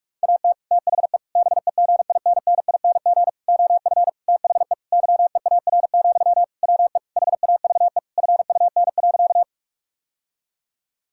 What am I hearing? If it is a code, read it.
Morse code, heard as a keyed tone, about 46 wpm, 700 Hz. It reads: AT THE BEGINNING OF THE YEAR, WE HAVE FUN.
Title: Sentences from Top 1000 Words - Review Part 5 - 45wpm